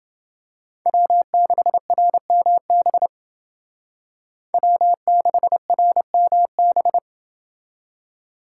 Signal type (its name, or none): Morse code